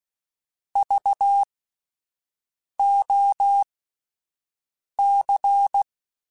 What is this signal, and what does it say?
Morse code, heard as a keyed tone, about 16 wpm, spaced as for 5 wpm, 800 Hz. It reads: VOC
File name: (spaced) / QSOs Spaced-mp3s / WD0BGP 1 @10 sp